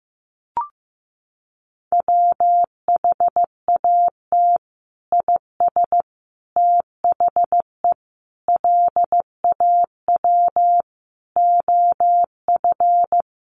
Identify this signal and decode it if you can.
Morse code, heard as a keyed tone, about 15 wpm, 700 Hz. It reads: WHAT IS THE LAW OF
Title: Sentences from Top 1300 Words 15wpm